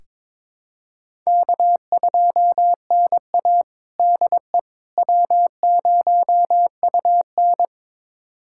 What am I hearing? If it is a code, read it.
Morse code, heard as a keyed tone, about 21 wpm, 700 Hz. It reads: K2NA DE W0UN
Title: USA Callsign DE Callsign - 4 Characters - 22wpm